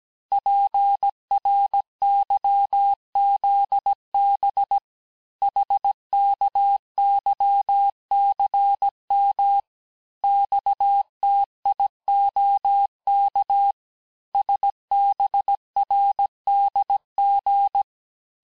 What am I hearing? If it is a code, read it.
Morse code, heard as a keyed tone, about 17 wpm, 800 Hz. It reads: PRYZB HKYCM XTIOK SBRDG
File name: corrige-cours-1581-texte-chiffre-vitesse-16-mpm